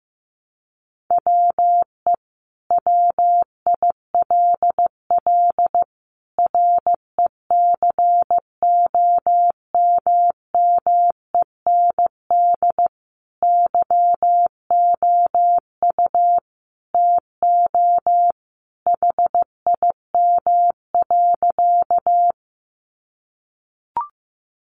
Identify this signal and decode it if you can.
Morse code, heard as a keyed tone, about 15 wpm, 700 Hz. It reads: WE WILL RECOMMEND YOU TO HIM.
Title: Sentences from Top 1200 Words 15wpm